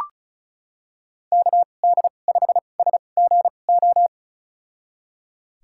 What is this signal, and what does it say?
Morse code, heard as a keyed tone, about 34 wpm, spaced as for 25 wpm, 700 Hz. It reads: KD5SGO
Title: USA Call Signs - 6 Characters - 2x Character Spacing - 35wpm